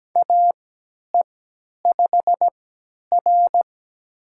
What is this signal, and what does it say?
Morse code, heard as a keyed tone, about 17 wpm, spaced as for 10 wpm, 700 Hz. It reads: AE5R
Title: USA Call Signs - 4 Characters - 3x Character Spacing - 17wpm